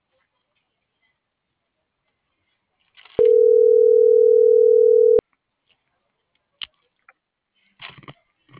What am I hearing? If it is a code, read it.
Ringback tone (North America), 440 + 480 Hz.